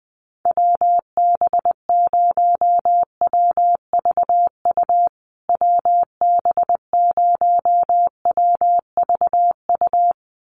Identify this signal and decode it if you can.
Morse code, heard as a keyed tone, about 19 wpm, 700 Hz. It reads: WB0WVU WB0WVU
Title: USA Call Signs Repeated 2x 20wpm